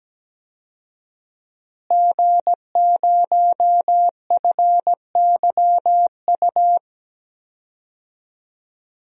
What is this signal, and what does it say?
Morse code, heard as a keyed tone, about 17 wpm, 700 Hz. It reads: G0FYU
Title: DXCC-100 Call Signs - 5 Characters - Repeated 2x - 17wpm